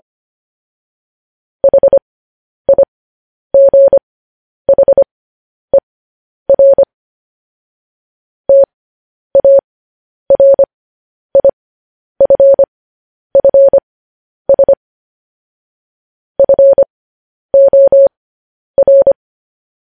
Morse code, heard as a keyed tone, about 25 wpm, spaced as for 10 wpm, 550 Hz. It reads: HIGHER TARIFFS FOR
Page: news headlines in morse code at 10 wpm